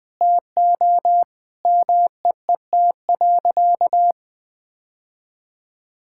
Morse code, heard as a keyed tone, about 20 wpm, 700 Hz. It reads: TO MEET.
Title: Sentences with 3-Word Phrases - 20wpm